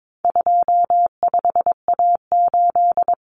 Morse code, heard as a keyed tone, about 22 wpm, 700 Hz. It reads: 25A8
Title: Common Cut Numbers - 4-Digit 22wpm